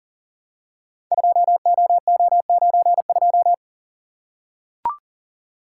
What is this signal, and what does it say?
Morse code, heard as a keyed tone, about 40 wpm, 700 Hz. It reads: 2OO92
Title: Code Groups - 5 Letter:Numbers - 40wpm